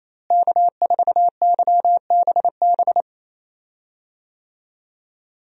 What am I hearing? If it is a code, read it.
Morse code, heard as a keyed tone, about 28 wpm, 700 Hz. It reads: K4YBB
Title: USA Call Signs - 5 Characters - Repeated 2x - 28wpm